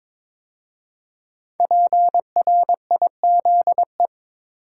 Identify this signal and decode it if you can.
Morse code, heard as a keyed tone, about 22 wpm, 700 Hz. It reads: PRIZE
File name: 5 Letter Words - First 3 Letters Given - 22wpm